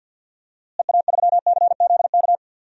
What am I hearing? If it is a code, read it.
Morse code, heard as a keyed tone, about 49 wpm, 700 Hz. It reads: EA3CZK
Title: DXCC-100 Call Signs - 6 Characters - Repeated 2x - 50wpm